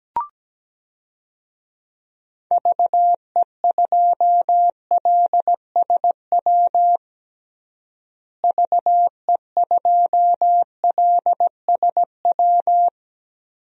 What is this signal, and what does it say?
Morse code, heard as a keyed tone, about 17 wpm, 700 Hz. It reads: VE2LSW VE2LSW
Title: DXCC-100 Call Signs - 6 Characters - Repeated 2x - 17wpm